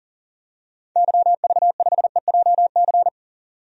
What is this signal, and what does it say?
Morse code, heard as a keyed tone, about 40 wpm, 700 Hz. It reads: YV5EJC
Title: DXCC-100 Call Signs - 6 Characters - 40wpm